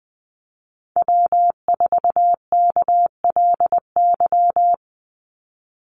Morse code, heard as a keyed tone, about 20 wpm, 700 Hz. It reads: W4KLY
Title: Call Signs Repeated 2x - 20wpm